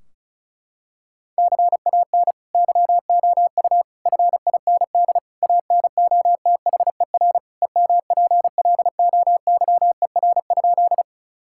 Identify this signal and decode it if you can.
Morse code, heard as a keyed tone, about 34 wpm, 700 Hz. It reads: CAN YOU FIND ANOTHER EMPLOYER?
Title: Sentences from Top 1600 Words 35wpm